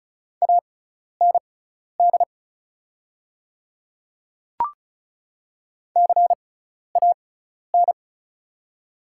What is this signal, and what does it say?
Morse code, heard as a keyed tone, about 34 wpm, spaced as for 12 wpm, 690 Hz. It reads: AND CAN
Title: 6x Character Spacing - Top 200 Words - 35wpm